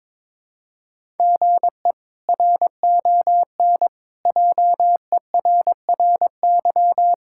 Morse code, heard as a keyed tone, about 22 wpm, 700 Hz. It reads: GE RON JERRY
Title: K1USN SST QSO Practice - 22wpm